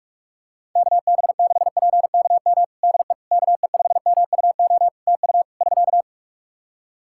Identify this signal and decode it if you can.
Morse code, heard as a keyed tone, about 46 wpm, 700 Hz. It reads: KB6PKM DE KE5MUO TU <SK>
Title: Practice QSO - Exchange 4 - Ending - 45wpm